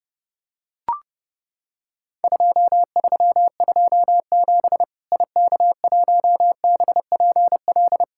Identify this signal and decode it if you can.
Morse code, heard as a keyed tone, about 30 wpm, 700 Hz. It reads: 2327 IK1BPL
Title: Worked All Europe DX Contest - QTC Exchange - 30wpm